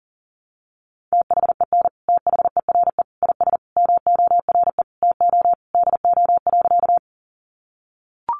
Morse code, heard as a keyed tone, about 40 wpm, 700 Hz. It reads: THEN THERE IS MORE TO DO.
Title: Sentences from Top 100 Words 40wpm